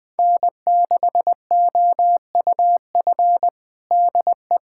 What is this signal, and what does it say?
Morse code, heard as a keyed tone, about 20 wpm, 700 Hz. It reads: N6OUF DE
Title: USA Callsign DE Callsign - 5 Characters - 20wpm